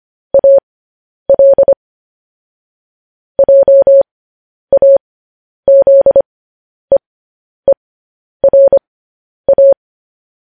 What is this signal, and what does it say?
Morse code, heard as a keyed tone, about 25 wpm, spaced as for 10 wpm, 550 Hz. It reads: AL JAZEERA